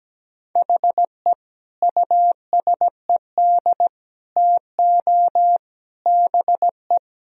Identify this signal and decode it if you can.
Morse code, heard as a keyed tone, about 17 wpm, 700 Hz. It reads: HE USED TO BE
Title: Sentences from Top 1000 Words - Review Part 4 - 17wpm